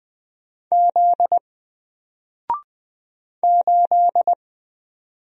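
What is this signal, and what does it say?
Morse code, heard as a keyed tone, about 20 wpm, 700 Hz. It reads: Z 8
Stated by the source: Single letter and number Morse drill at 20 wpm, from N4LCD.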